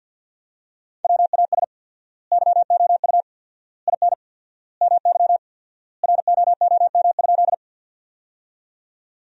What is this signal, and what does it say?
Morse code, heard as a keyed tone, about 50 wpm, 700 Hz. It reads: WAS YOU IN MY ROOM?
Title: 4x Word-Spacing - Sentences from Top 300 Words - 50wpm